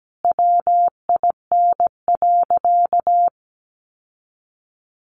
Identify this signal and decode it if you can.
Morse code, heard as a keyed tone, about 17 wpm, 700 Hz. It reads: WIN.
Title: Sentences from Top 1500 Words 17wpm